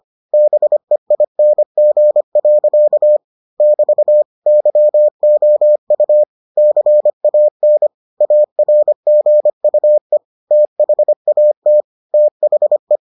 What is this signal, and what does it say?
Morse code, heard as a keyed tone, about 25 wpm, 600 Hz. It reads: BEING. <BT> YOU CAN ARGUE THAT THE